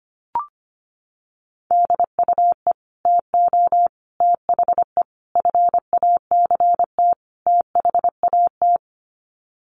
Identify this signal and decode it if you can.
Morse code, heard as a keyed tone, about 25 wpm, 700 Hz. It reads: DUE TO THE FACT THAT
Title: Sets of 5 Words Encore 25wpm